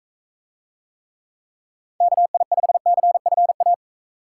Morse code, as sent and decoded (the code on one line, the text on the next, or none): -.- .. ..... -.-. ..-. .-
KI5CFA